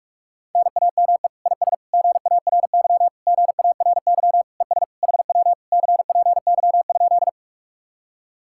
Morse code, heard as a keyed tone, about 49 wpm, 700 Hz. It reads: NAME IS GARY GARY ES HW CPY?